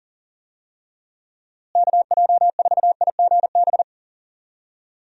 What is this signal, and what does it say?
Morse code, heard as a keyed tone, about 40 wpm, 700 Hz. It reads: KJ4IGB
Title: USA Call Signs - 6 Characters - Repeated 3x - 40wpm